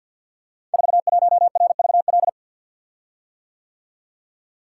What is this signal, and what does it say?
Morse code, heard as a keyed tone, about 53 wpm, 700 Hz. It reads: 41RVL
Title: Code Groups - 5 Letter:Numbers - 50wpm